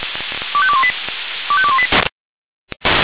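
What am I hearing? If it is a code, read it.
Five-tone selective call (ZVEI), heard as quick runs of five tones, tones of about 70 ms. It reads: 25218 25218
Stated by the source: Five-tone ZVEI selective call received on SDR.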